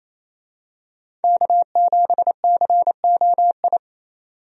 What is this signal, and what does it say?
Morse code, heard as a keyed tone, about 27 wpm, 700 Hz. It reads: K7COI
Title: USA Call Signs - 5 Characters - 28wpm